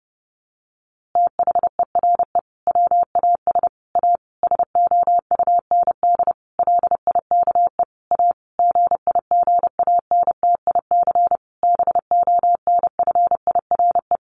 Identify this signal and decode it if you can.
Morse code, heard as a keyed tone, about 30 wpm, 700 Hz. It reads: THERE WAS A SOUND LIKE A GIGANTIC BONFIRE
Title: A Sound Of Thunder 30wpm